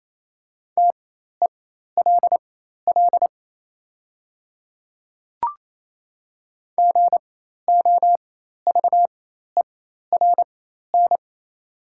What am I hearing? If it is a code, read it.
Morse code, heard as a keyed tone, about 28 wpm, spaced as for 13 wpm, 690 Hz. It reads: TELL GOVERN
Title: 4x Character Spacing - Top 500 Words - 28wpm